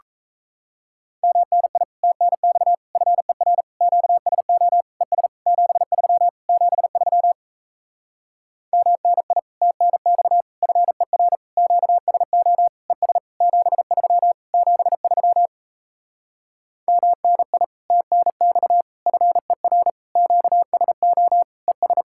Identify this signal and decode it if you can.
Morse code, heard as a keyed tone, about 37 wpm, 700 Hz. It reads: MNI TNX FER QSO ES 73 73 MNI TNX FER QSO ES 73 73 MNI TNX FER QSO ES